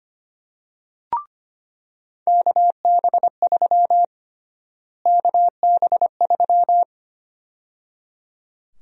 Morse code, heard as a keyed tone, about 25 wpm, 700 Hz. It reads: KB3 KB3
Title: USA Call Sign Prefixes - Repeated 2x - 25wpm